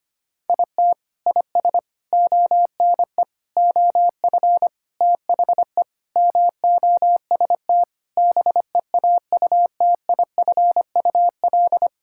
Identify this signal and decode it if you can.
Morse code, heard as a keyed tone, about 25 wpm, 700 Hz. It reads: IT IS ONE OF THE MOST BEAUTIFUL
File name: practice / 25wpm / Sentences with 3-Word Phrases - 25wpm